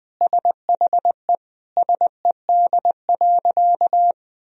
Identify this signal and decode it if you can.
Morse code, heard as a keyed tone, about 20 wpm, 700 Hz. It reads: SHE SED.
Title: Sentences from Top 400 Words and Abbreviations 20wpm